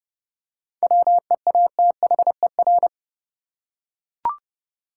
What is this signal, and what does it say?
Morse code, heard as a keyed tone, about 32 wpm, 700 Hz. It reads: WEATHER
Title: Seven Letter Words - 30wpm